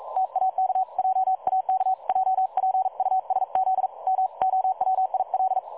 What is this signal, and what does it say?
Morse code, heard as a keyed tone, about 42 wpm, 760 Hz. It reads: TAK JAK JPUSZ MOWIL
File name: dlazaawansowanych / nagraniazpasma / websdr recording 3565.0kHz